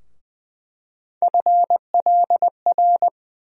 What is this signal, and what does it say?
Morse code, heard as a keyed tone, about 21 wpm, 700 Hz. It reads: FLR
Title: USA Call Sign Postfixes - 3 Letters 20wpm